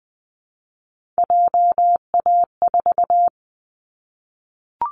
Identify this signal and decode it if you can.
Morse code, heard as a keyed tone, about 19 wpm, 700 Hz. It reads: JA4
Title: Callsign Prefixes for Top 50 DXCC - 20wpm